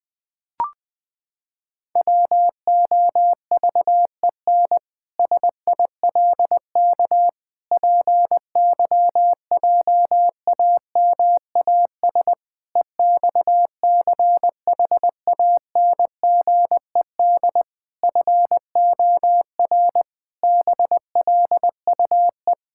Morse code, heard as a keyed tone, about 19 wpm, 700 Hz. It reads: WOVEN SILK PYJAMAS EXCHANGED FOR BLUE
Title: Pangram Challenge v2 - 20wpm